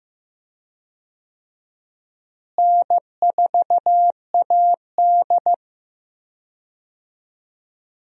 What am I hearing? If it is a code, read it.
Morse code, heard as a keyed tone, about 15 wpm, 700 Hz. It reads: N4AD
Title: USA Call Signs - 4 Characters - Repeated 3x - 15wpm